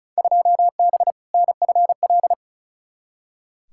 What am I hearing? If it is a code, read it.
Morse code, heard as a keyed tone, about 36 wpm, 690 Hz. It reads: 2B NFL
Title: Field Day Exchanges Encore - 35wpm